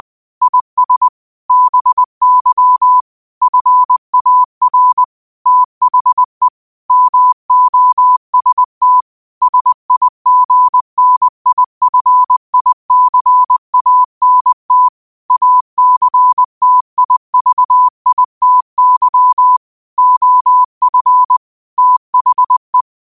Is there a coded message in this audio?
Yes: Morse code from 0.4 to 22.8 s, heard as a keyed tone, about 20 wpm, 1000 Hz. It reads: IS BY FAR THE MOST SIGNIFICANT ACTIVITY OF THE